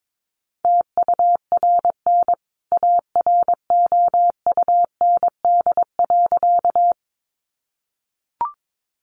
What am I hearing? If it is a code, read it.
Morse code, heard as a keyed tone, about 22 wpm, 700 Hz. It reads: TURN AROUND.
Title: A Sound Of Thunder 22wpm